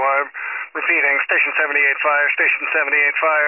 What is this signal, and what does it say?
Emergency Alert System (SAME) tones, heard as harsh buzzing bursts, with speech. It is heard from 0.3 to 3.1 s.